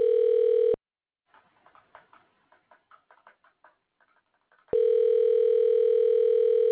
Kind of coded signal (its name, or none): telephone call progress tone